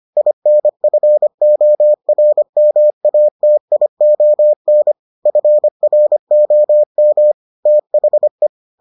Morse code, heard as a keyed tone, about 25 wpm, 600 Hz. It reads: INFORMATION FROM THE